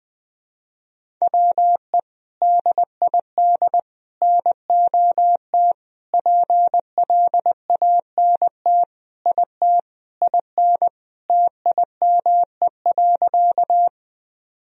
Morse code, heard as a keyed tone, about 21 wpm, 700 Hz. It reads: WE DID NOT PLANT IT IN TIME.
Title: Sentences from Top 1000 Words - Review Part 1 - 20wpm